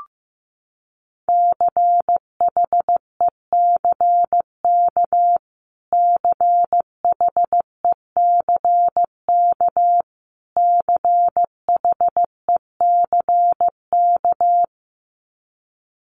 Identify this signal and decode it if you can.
Morse code, heard as a keyed tone, about 14 wpm, 700 Hz. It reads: CHECK CHECK CHECK
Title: Top 500 Words Repeated 15wpm